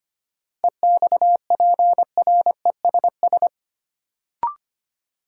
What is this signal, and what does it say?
Morse code, heard as a keyed tone, about 25 wpm, 700 Hz. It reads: EXPRESS
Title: Sentences from Top 1100 Words - New Word Drill - 25wpm